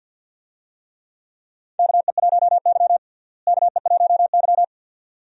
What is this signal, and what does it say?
Morse code, heard as a keyed tone, about 51 wpm, 700 Hz. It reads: KE1Y KE1Y
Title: USA Call Signs - 4 Characters - Repeated 2x - 50wpm